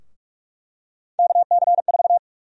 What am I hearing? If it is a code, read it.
Morse code, heard as a keyed tone, about 41 wpm, 700 Hz. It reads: KC4
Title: USA Call Sign Prefixes - Repeated 2x - 45wpm